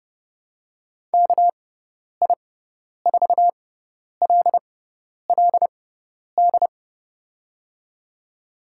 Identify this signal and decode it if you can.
Morse code, heard as a keyed tone, about 29 wpm, spaced as for 10 wpm, 700 Hz. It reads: KI4LLD
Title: USA Call Signs - 6 Characters - 6x Character Spacing - 30wpm